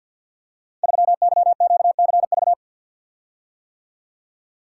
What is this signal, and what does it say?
Morse code, heard as a keyed tone, about 49 wpm, 700 Hz. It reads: 3YQCV
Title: Code Groups - 5 Letter:Numbers - 50wpm